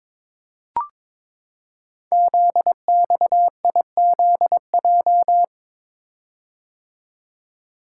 Morse code, heard as a keyed tone, about 22 wpm, 700 Hz. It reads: ZXIZJ
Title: Code Groups - 5 Letters - 22wpm